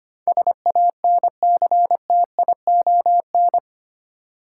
Morse code, heard as a keyed tone, about 25 wpm, 700 Hz. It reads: SANCTION
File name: Eight Letter Words - 25wpm